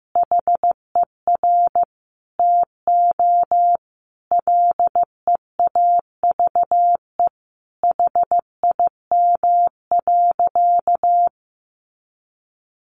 Morse code, heard as a keyed tone, about 15 wpm, 700 Hz. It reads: HER TO LEAVE HIM.
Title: Sentences from Top 1500 Words 15wpm